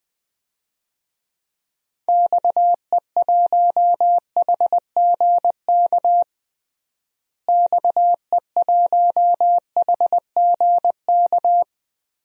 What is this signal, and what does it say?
Morse code, heard as a keyed tone, about 19 wpm, 700 Hz. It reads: XE1HGK XE1HGK
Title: DXCC-100 Call Signs - 6 Characters - Repeated 3x - 20wpm